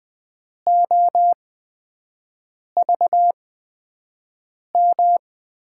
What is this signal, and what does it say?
Morse code, heard as a keyed tone, about 20 wpm, spaced as for 5 wpm, 700 Hz. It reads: OVM
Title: USA Call Signs - 6 Characters - 8x Character Spacing - 20wpm